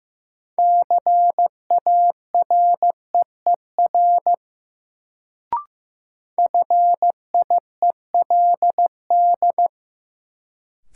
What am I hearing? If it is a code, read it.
Morse code, heard as a keyed tone, about 15 wpm, 700 Hz. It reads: CAREER FIELD
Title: Sentences from Top 1000 Words - New Word Drill - 15wpm